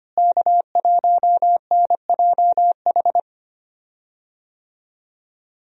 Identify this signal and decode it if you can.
Morse code, heard as a keyed tone, about 25 wpm, 700 Hz. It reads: K1NJH
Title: USA Call Signs - 5 Characters - Repeated 3x - 25wpm